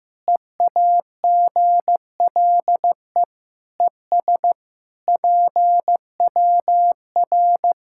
Morse code, heard as a keyed tone, about 15 wpm, 700 Hz. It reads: EAGLE ES PWR
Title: Practice QSO - Exchange 2 - Info - 15wpm